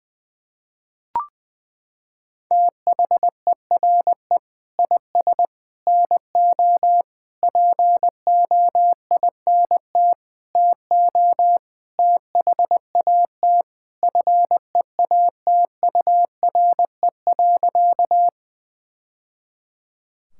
Morse code, heard as a keyed tone, about 20 wpm, 700 Hz. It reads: THERE IS NO POINT TO THAT FEATURE.
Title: Sentences from Top 1000 Words - Review Part 5 - 20wpm